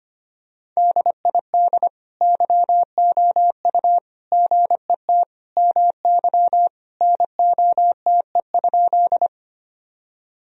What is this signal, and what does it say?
Morse code, heard as a keyed tone, about 25 wpm, 700 Hz. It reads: DID YOU GET MY NOTE?
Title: Sentences from Top 1000 Words - Review Part 2 - 25wpm